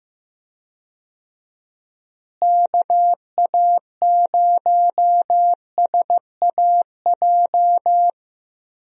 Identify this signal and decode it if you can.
Morse code, heard as a keyed tone, about 15 wpm, 700 Hz. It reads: KA0SAJ
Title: USA Call Signs - 6 Characters - Repeated 3x - 15wpm